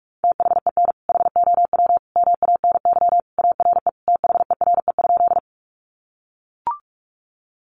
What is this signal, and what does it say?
Morse code, heard as a keyed tone, about 44 wpm, 700 Hz. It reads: THEN HOW MANY ARE THERE?
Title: Sentences from Top 100 Words 45wpm